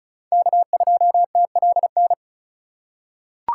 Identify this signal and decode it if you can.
Morse code, heard as a keyed tone, about 36 wpm, 700 Hz. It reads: K2TLN
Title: USA Call Signs - 5 Characters - 4x Character Spacing - 35wpm